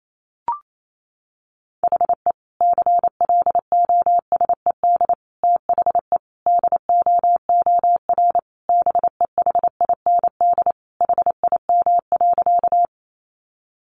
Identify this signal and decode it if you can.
Morse code, heard as a keyed tone, about 28 wpm, 700 Hz. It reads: HE CLOSED THE DOOR BEHIND HIM.
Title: Sentences from Top 1400 Words 28wpm